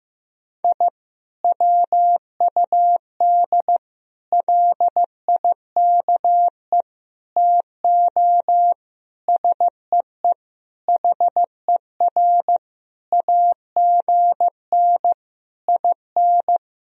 Morse code, heard as a keyed tone, about 15 wpm, 700 Hz. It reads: I WUD LIKE TO SEE HER AGN IN
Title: Sentences from Top 200 Words and Abbreviations 15wpm